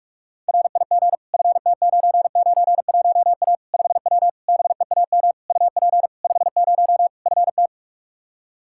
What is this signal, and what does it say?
Morse code, heard as a keyed tone, about 46 wpm, 700 Hz. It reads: RIG FT991A 5W BEAM UP 50 FT
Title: Practice QSO - Exchange 2 - Info - 45wpm